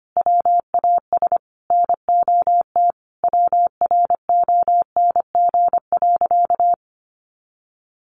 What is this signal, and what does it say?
Morse code, heard as a keyed tone, about 25 wpm, 700 Hz. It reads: WAS NOT WRONG.
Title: Sentences from Top 900 Words 25wpm